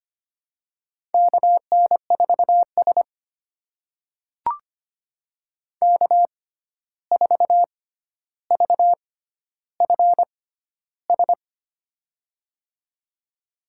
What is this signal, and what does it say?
Morse code, heard as a keyed tone, about 25 wpm, spaced as for 12 wpm, 700 Hz. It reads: KN4S K4VFS
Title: USA Call Signs - 6x Character Spacing - 25wpm